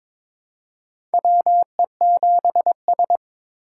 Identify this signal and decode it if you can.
Morse code, heard as a keyed tone, about 22 wpm, 700 Hz. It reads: WE7S
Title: USA Call Signs - 4 Characters - 3x Character Spacing - 22wpm